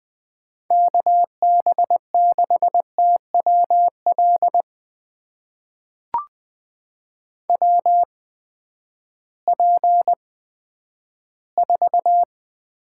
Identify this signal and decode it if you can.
Morse code, heard as a keyed tone, about 20 wpm, spaced as for 11 wpm, 700 Hz. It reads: KB6TWL WP4
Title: USA Call Signs - 6 Characters - 8x Character Spacing - 20wpm